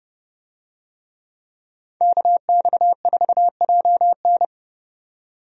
Morse code, heard as a keyed tone, about 30 wpm, 700 Hz. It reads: KX4JN
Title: USA Call Signs - 5 Characters - 30wpm